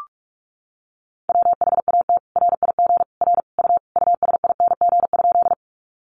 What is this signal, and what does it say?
Morse code, heard as a keyed tone, about 44 wpm, 700 Hz. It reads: WHAT RIG R U USING?
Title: Common QSO Phrases 45wpm